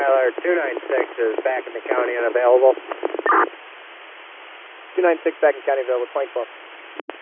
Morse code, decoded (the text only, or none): none